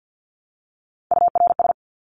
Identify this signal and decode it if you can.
Morse code, heard as a keyed tone, about 50 wpm, 700 Hz. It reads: URS
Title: Abbreviations Repeated 3x - 50wpm